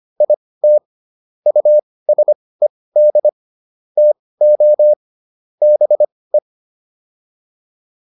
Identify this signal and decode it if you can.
Morse code, heard as a keyed tone, about 25 wpm, spaced as for 18 wpm, 600 Hz. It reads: IT USED TO BE